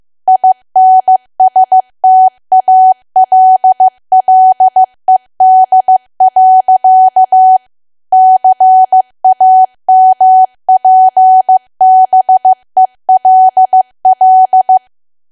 Morse code, heard as a keyed tone, about 15 wpm, 750 Hz. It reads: INSTALLED. CAMPBELL
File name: cwa / W1AW-Level2 / 15WPM